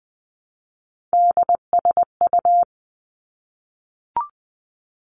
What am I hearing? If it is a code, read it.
Morse code, heard as a keyed tone, about 20 wpm, 700 Hz. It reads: DSU
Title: Three Challenging Letter-Number Combinations 20wpm